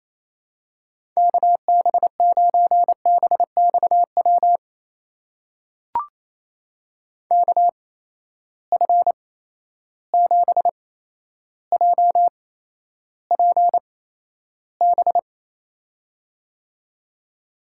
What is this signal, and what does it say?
Morse code, heard as a keyed tone, about 28 wpm, spaced as for 12 wpm, 700 Hz. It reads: KB9BXW KF7JPB
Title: USA Call Signs - 6 Characters - 8x Character Spacing - 28wpm